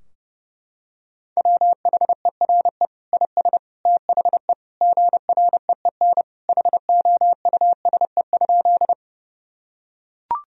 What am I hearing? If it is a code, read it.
Morse code, heard as a keyed tone, about 30 wpm, 700 Hz. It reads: WHERE IS THE GREEN HOUSE?
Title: Sentences from Top 1000 Words - Review Part 2 - 30wpm